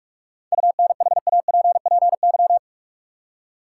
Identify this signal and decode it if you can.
Morse code, heard as a keyed tone, about 44 wpm, 700 Hz. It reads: UNHAPPY
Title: Seven Letter Words - 45wpm